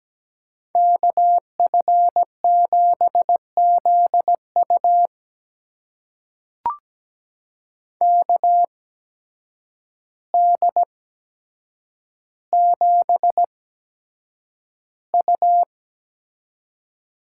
Morse code, heard as a keyed tone, about 17 wpm, spaced as for 8 wpm, 700 Hz. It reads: KF7ZU KD7U
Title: USA Call Signs - 5 Characters - 8x Character Spacing - 17wpm